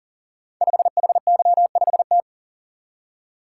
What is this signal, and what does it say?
Morse code, heard as a keyed tone, about 42 wpm, 700 Hz. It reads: 5HY5T
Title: Code Groups - 5 Letter:Numbers - 40wpm